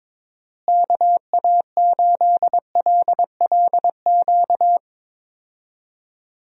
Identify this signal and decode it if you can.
Morse code, heard as a keyed tone, about 22 wpm, 700 Hz. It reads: KA8LLQ
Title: USA Call Signs - 6 Characters - Repeated 3x - 22wpm